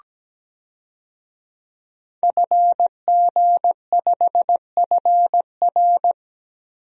Morse code, heard as a keyed tone, about 17 wpm, 700 Hz. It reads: FG5FR
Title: DXCC-100 Call Signs - 5 Characters - Repeated 2x - 17wpm